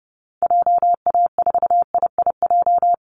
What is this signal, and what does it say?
Morse code, heard as a keyed tone, about 30 wpm, 700 Hz. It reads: JA4IIJ
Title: Call Signs - 30wpm